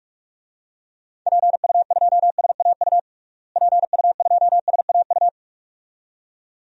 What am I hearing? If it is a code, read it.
Morse code, heard as a keyed tone, about 46 wpm, 700 Hz. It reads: PU2SAU PU2SAU
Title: DXCC-100 Call Signs - 6 Characters - Repeated 2x - 45wpm